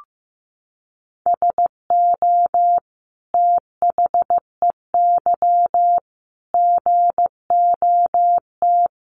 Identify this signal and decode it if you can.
Morse code, heard as a keyed tone, about 15 wpm, 700 Hz. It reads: SO THEY GOT
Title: Sentences from Top 300 Words 15wpm